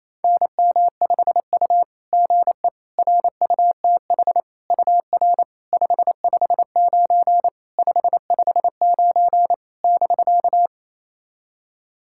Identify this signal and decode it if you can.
Morse code, heard as a keyed tone, about 28 wpm, 690 Hz. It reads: NM5U GE RUTH UR 559 559 <BK>
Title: SOTA QSO Practice 28wpm